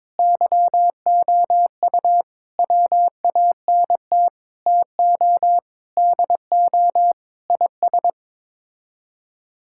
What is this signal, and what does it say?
Morse code, heard as a keyed tone, about 22 wpm, 700 Hz. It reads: YOU WANT TO DO IS
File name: practice / 22wpm / Sets of 8 Words - 22wpm